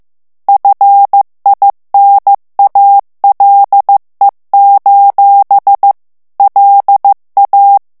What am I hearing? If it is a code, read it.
Morse code, heard as a keyed tone, about 15 wpm, 800 Hz. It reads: FINALE: LA
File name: morse mp3 / ITALIEN texte clair 2 vitesse 14 mots minute